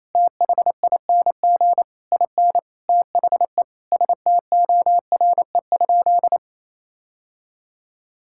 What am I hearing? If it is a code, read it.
Morse code, heard as a keyed tone, about 28 wpm, 700 Hz. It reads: THING IN THE STORE?
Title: Sentences from Top 1000 Words - Review Part 6 - 28wpm